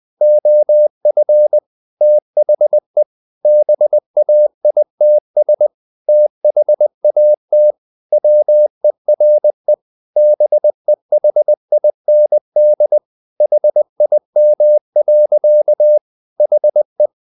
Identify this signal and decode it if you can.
Morse code, heard as a keyed tone, about 20 wpm, 600 Hz. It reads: OF THE BAITS THAT WERE BEHIND HIM. HE
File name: the old man and the sea / the old man and the sea part 1 20wpm